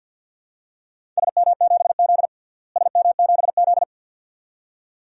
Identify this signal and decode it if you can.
Morse code, heard as a keyed tone, about 52 wpm, 700 Hz. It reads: SM7Z SM7Z
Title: DXCC-100 Call Signs - 4 Characters - Repeated 2x - 50wpm